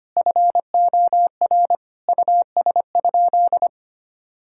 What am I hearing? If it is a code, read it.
Morse code, heard as a keyed tone, about 25 wpm, 700 Hz. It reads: FOR US?
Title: Sentences from Top 1000 Words - Review Part 5 - 25wpm